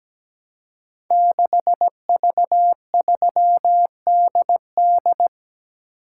Morse code, heard as a keyed tone, about 17 wpm, 700 Hz. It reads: 6V3DD
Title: Code Groups - 5 Letter:Numbers - 17wpm